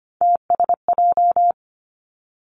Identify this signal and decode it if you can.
Morse code, heard as a keyed tone, about 25 wpm, 700 Hz. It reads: TSJ
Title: Three Challenging Letter-Number Combinations 25wpm